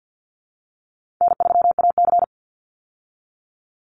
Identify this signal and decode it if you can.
Morse code, heard as a keyed tone, about 50 wpm, 700 Hz. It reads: N3AC
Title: Call Signs Repeated 2x - 50wpm